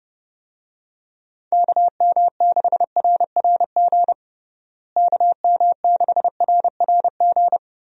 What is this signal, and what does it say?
Morse code, heard as a keyed tone, about 32 wpm, 700 Hz. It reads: KM6RRG KM6RRG
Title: USA Call Signs - 6 Characters - Repeated 2x - 30wpm